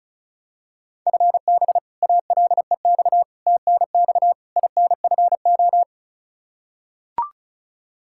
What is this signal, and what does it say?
Morse code, heard as a keyed tone, about 34 wpm, 700 Hz. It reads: FB ALEX TNX INFO
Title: Practice QSO - Exchange 3 - More Info - 35wpm